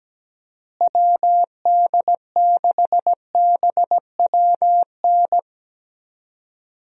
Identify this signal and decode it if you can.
Morse code, heard as a keyed tone, about 17 wpm, 700 Hz. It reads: WD6BWN
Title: USA Call Signs - 6 Characters - Repeated 2x - 17wpm